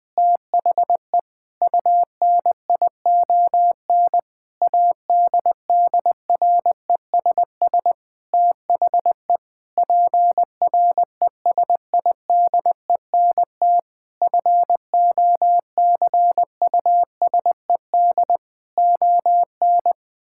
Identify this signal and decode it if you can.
Morse code, heard as a keyed tone, about 20 wpm, 700 Hz. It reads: THE UNION ADDRESS THE PRESIDENT FOCUSED ON
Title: Sentences with 7-Word Phrases - Spoken Beginning - 20wpm